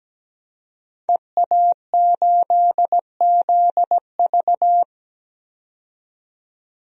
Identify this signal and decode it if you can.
Morse code, heard as a keyed tone, about 17 wpm, 700 Hz. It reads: EA8ZV
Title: DXCC-100 Call Signs - 5 Characters - 17wpm